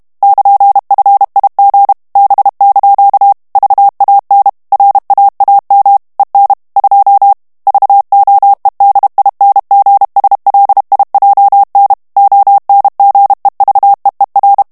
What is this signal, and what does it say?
Morse code, heard as a keyed tone, about 32 wpm, 800 Hz. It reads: <KN>FIG B) VAN RAAM EN 2 VOEDINGSLIJN ONGEVEER
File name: HOLLANDAIS texte clair 1 vitesse 30 mots minute